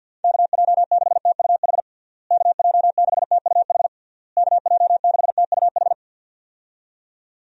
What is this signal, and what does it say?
Morse code, heard as a keyed tone, about 52 wpm, 700 Hz. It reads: KJ6TUH KJ6TUH KJ6TUH
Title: USA Call Signs - 6 Characters - Repeated 3x - 50wpm